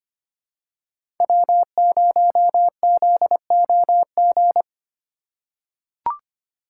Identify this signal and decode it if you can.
Morse code, heard as a keyed tone, about 25 wpm, 700 Hz. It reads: W0ZOG
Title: USA Call Signs - 5 Characters - 4x Character Spacing - 25wpm